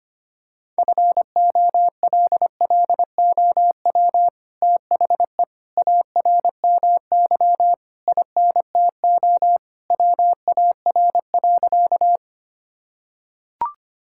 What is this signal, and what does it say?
Morse code, heard as a keyed tone, about 25 wpm, 700 Hz. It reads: FOLLOW THE ARMY INTO WAR.
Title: Sentences from Top 1000 Words - Review Part 5 - 25wpm